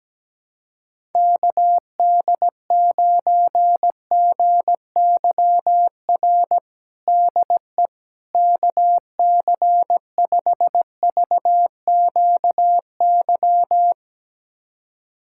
Morse code, heard as a keyed tone, about 17 wpm, 700 Hz. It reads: KD9GYR DE KC5VQY
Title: USA Callsign DE Callsign - 6 Characters - 17wpm